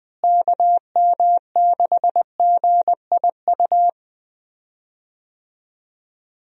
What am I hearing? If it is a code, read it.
Morse code, heard as a keyed tone, about 20 wpm, 700 Hz. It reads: KM6GIU
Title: USA Call Signs - 6 Characters - Repeated 2x - 20wpm